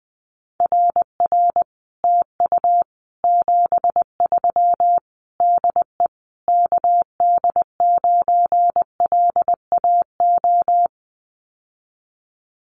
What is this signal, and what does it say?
Morse code, heard as a keyed tone, about 20 wpm, 700 Hz. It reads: RR TU 73 DE KD9LAO
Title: Common QSO Phrases Encore 20wpm